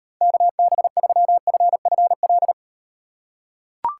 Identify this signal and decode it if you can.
Morse code, heard as a keyed tone, about 38 wpm, 700 Hz. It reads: KB3FFL